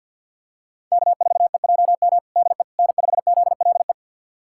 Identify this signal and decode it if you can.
Morse code, heard as a keyed tone, about 51 wpm, 690 Hz. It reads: K4EJM DE N5ZLE
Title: USA Callsign DE Callsign - 5 Characters - 50wpm